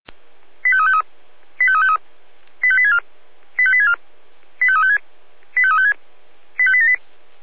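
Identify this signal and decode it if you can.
Five-tone selective call (ZVEI), heard as quick runs of five tones, tones of about 70 ms. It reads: 85353 85353 85864 85864 85356 85356 85878
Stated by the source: ZVEI five-tone selective call received on an SDR.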